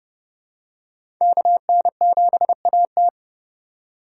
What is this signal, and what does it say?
Morse code, heard as a keyed tone, about 30 wpm, 700 Hz. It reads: KN7AT
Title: USA Call Signs - 3x Character Spacing - 30wpm